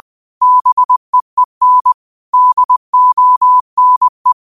Morse code, heard as a keyed tone, about 20 wpm, 1000 Hz. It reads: BEEN DONE